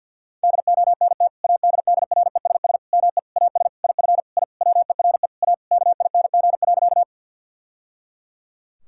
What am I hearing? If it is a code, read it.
Morse code, heard as a keyed tone, about 49 wpm, 700 Hz. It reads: DONT ADDRESS ME AS IF I WERE A KING.
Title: Sentences from Top 1000 Words - Review Part 6 - 50wpm